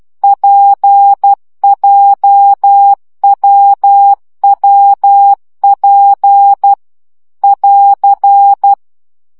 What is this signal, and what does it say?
Morse code, heard as a keyed tone, about 12 wpm, 800 Hz. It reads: PJWWP <AR>
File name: CW gyakorlas w p j 10 baud